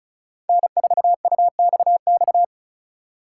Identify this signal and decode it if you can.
Morse code, heard as a keyed tone, about 36 wpm, 700 Hz. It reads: N4UXX
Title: USA Call Signs - 2x Character Spacing - 35wpm